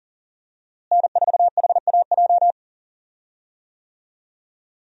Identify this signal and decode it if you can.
Morse code, heard as a keyed tone, about 37 wpm, 700 Hz. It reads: N4HAJ
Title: USA Call Signs - 5 Characters - Repeated 3x - 40wpm